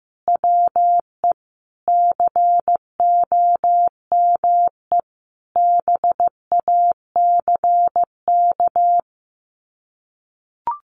Morse code, heard as a keyed tone, about 15 wpm, 700 Hz. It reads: WE COME BACK
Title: Sets of 5 Words 15wpm